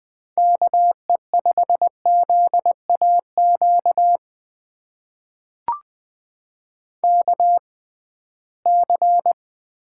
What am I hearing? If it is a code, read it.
Morse code, heard as a keyed tone, about 20 wpm, spaced as for 15 wpm, 700 Hz. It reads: KE5ZAQ KC
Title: USA Call Signs - 6 Characters - 6x Character Spacing - 20wpm